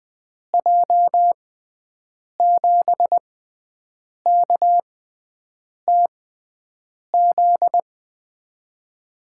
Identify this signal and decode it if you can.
Morse code, heard as a keyed tone, about 20 wpm, spaced as for 7 wpm, 700 Hz. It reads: J7KTZ
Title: USA Call Signs - 6x Character Spacing - 20wpm